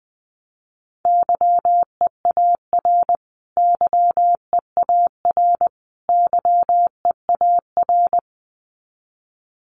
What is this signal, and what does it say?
Morse code, heard as a keyed tone, about 20 wpm, 700 Hz. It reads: YEAR YEAR YEAR
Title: Top 500 Words Repeated 20wpm